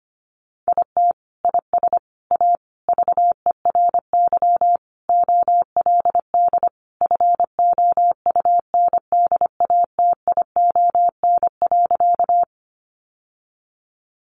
Morse code, heard as a keyed tone, about 25 wpm, 700 Hz. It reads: IT IS A VERY OLD FOUNDATION.
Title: Sentences from Top 1200 Words 25wpm